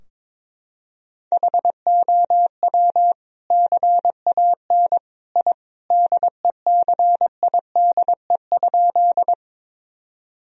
Morse code, heard as a keyed tone, about 22 wpm, 700 Hz. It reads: HOW CAN I DECIDE?
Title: Sentences from Top 1000 Words - Review Part 3 - 22wpm